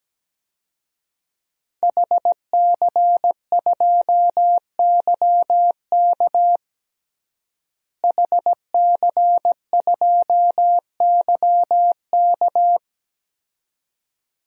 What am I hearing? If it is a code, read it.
Morse code, heard as a keyed tone, about 17 wpm, 700 Hz. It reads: HC2YK HC2YK
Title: DXCC-100 Call Signs - 5 Characters - Repeated 2x - 17wpm